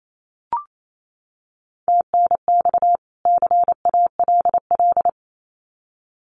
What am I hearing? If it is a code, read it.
Morse code, heard as a keyed tone, about 28 wpm, 700 Hz. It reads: TNX CALL
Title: Common QSO Phrases 28wpm